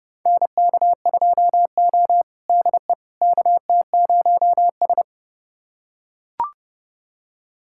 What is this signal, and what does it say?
Morse code, heard as a keyed tone, about 30 wpm, 700 Hz. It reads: NK2O DE KT0S
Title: USA Callsign DE Callsign - 4 Characters - 30wpm